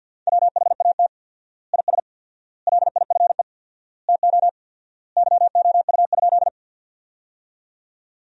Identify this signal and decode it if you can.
Morse code, heard as a keyed tone, about 50 wpm, 700 Hz. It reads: WHAT IS LIFE TO YOU?
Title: 4x Word-Spacing - Sentences from Top 300 Words - 50wpm